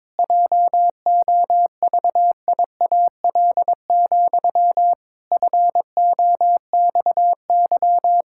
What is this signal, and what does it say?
Morse code, heard as a keyed tone, about 22 wpm, 700 Hz. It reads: JOVIAL, FOXY
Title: Pangram Challenge v2 - 22wpm